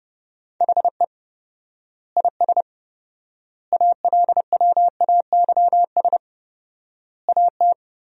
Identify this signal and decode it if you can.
Morse code, heard as a keyed tone, about 30 wpm, 700 Hz. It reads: HE IS ALWAYS AT
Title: 4x Word-Spacing - Sentences from Top 300 Words - 30wpm